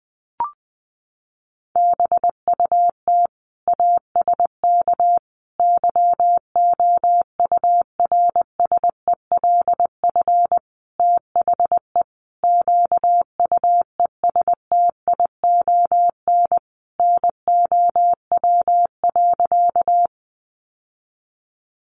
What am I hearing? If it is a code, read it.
Morse code, heard as a keyed tone, about 20 wpm, 700 Hz. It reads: BUT ASK YOURSELF THE QUESTION NOW.
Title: Sentences from Top 600 Words 20wpm